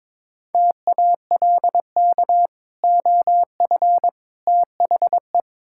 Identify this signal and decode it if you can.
Morse code, heard as a keyed tone, about 22 wpm, 700 Hz. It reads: TALK OF THE
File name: practice / 22wpm / Build A Phrase - 5-Word Phrases - 22wpm